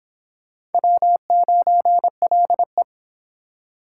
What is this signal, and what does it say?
Morse code, heard as a keyed tone, about 32 wpm, 700 Hz. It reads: W9LE